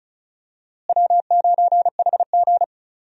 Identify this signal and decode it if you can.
Morse code, heard as a keyed tone, about 36 wpm, 700 Hz. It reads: W9HG
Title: USA Call Signs - 4 Characters - Repeated 3x - 35wpm